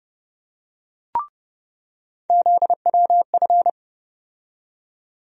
Morse code, heard as a keyed tone, about 30 wpm, 700 Hz. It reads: ZWF
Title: USA Call Sign Postfixes - 3 Letters 30wpm